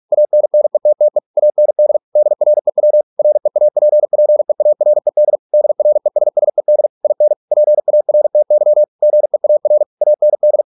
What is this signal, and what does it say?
Morse code, heard as a keyed tone, about 46 wpm, 600 Hz. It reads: ANNETTE AND DREW REAPPEARED DRESSED IN PARTY GEAR AND